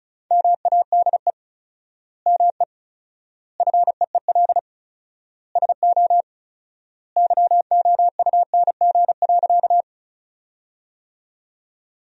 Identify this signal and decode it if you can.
Morse code, heard as a keyed tone, about 36 wpm, 700 Hz. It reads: MADE ME FEEL SO YOUNG.
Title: 4x Word-Spacing - Sentences from Top 400 Words - 35wpm